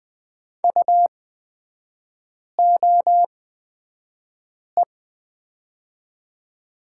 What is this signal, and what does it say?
Morse code, heard as a keyed tone, about 20 wpm, spaced as for 5 wpm, 700 Hz. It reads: UOE
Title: 224 Lesson 36e - BT - Three Characters 20wpm